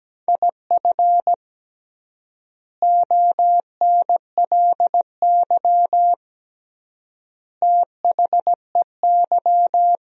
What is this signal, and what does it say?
Morse code, heard as a keyed tone, about 17 wpm, 700 Hz. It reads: IF ONLY THEY
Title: 3x Word-Spacing - Sentences from Top 500 Words - 17wpm